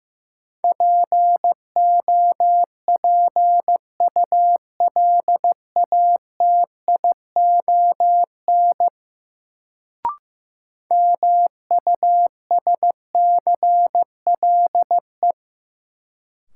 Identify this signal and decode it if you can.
Morse code, heard as a keyed tone, about 15 wpm, 700 Hz. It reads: POPULATION MUSCLE
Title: Sentences from Top 1300 Words - New Word Drill Alt - 15wpm